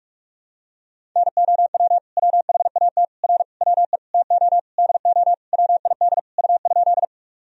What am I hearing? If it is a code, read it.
Morse code, heard as a keyed tone, about 46 wpm, 700 Hz. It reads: NOW WHAT R WE TO DO WID U?
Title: Sentences from Top 100 Words and Abbreviations 45wpm